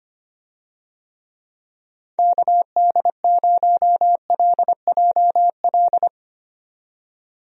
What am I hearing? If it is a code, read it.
Morse code, heard as a keyed tone, about 25 wpm, 700 Hz. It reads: KD0LJL
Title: USA Call Signs - 6 Characters - 25wpm